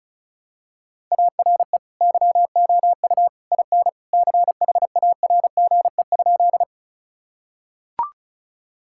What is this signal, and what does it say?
Morse code, heard as a keyed tone, about 34 wpm, 700 Hz. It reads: ARE YOU IN CHARGE?
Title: Sentences from Top 1000 Words - Review Part 6 - 35wpm